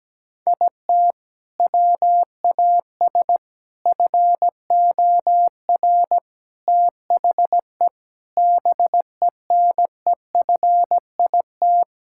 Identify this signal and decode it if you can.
Morse code, heard as a keyed tone, about 17 wpm, 700 Hz. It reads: IT WAS FOR THE BENEFIT
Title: Sentences from Top 1000 Words - Review Part 6 - 17wpm